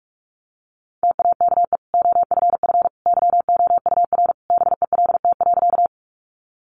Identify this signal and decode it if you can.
Morse code, heard as a keyed tone, about 44 wpm, 700 Hz. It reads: TAKE OFF YOUR BELT.
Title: Sentences from Top 1400 Words 45wpm